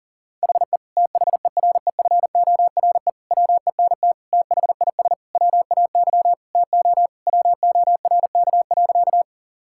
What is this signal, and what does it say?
Morse code, heard as a keyed tone, about 40 wpm, 690 Hz. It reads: HE THEREFORE WENT THIS WAY TO WORK.
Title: Sentences from Top 1000 Words - Review Part 5 - 40wpm